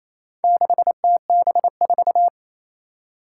Morse code, heard as a keyed tone, about 27 wpm, 700 Hz. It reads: 6TB4
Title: Four Challenging Letter-Number Combinations 28wpm